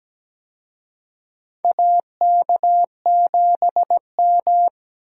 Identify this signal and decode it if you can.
Morse code, heard as a keyed tone, about 17 wpm, 700 Hz. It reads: AK7M
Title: USA Call Signs - 4 Characters - Repeated 2x - 17wpm